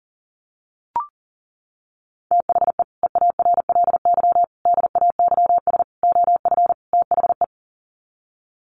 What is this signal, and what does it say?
Morse code, heard as a keyed tone, about 40 wpm, 700 Hz. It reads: THE EARLY DAYS OF THE
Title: Sets of 5 Words Encore 40wpm